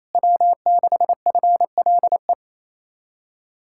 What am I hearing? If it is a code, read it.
Morse code, heard as a keyed tone, about 38 wpm, spaced as for 33 wpm, 700 Hz. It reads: W6FLE
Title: Speed-Racing Common QSO Phrases - 28wpm